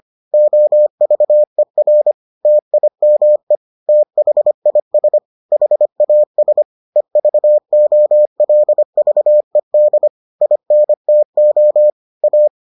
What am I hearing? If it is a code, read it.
Morse code, heard as a keyed tone, about 25 wpm, 600 Hz. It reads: OVER TIME THIS HAS EVOLVED INTO A